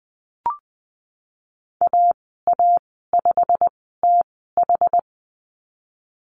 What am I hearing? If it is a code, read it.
Morse code, heard as a keyed tone, about 20 wpm, spaced as for 15 wpm, 700 Hz. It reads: AA5TH
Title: 088 Lesson 15g - Number 5 - Call Signs 20wpm